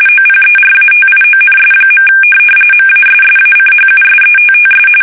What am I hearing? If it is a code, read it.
RTTY, about 45.45 baud. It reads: OPERATIONS. MASSACHUSETTS
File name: fsk850 / USS-Massachusetts